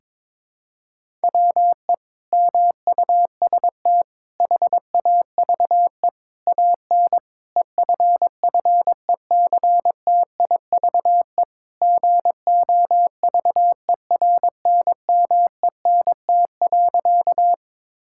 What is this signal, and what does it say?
Morse code, heard as a keyed tone, about 21 wpm, 700 Hz. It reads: WE MUST HAVE AN EFFECTIVE GOVERNMENT.
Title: Sentences from Top 1000 Words - Review Part 6 - 22wpm